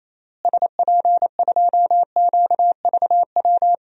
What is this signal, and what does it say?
Morse code, heard as a keyed tone, about 28 wpm, 700 Hz. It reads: SP2QVW
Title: DXCC-100 Call Signs - 6 Characters - Repeated 3x - 28wpm